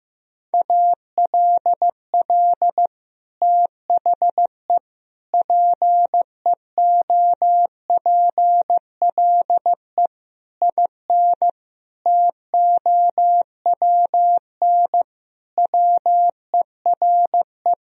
Morse code, heard as a keyed tone, about 16 wpm, 700 Hz. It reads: ALL THE PEOPLE IN TOWN WERE
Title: Sentences from Top 1000 Words - Review Part 4 - 15wpm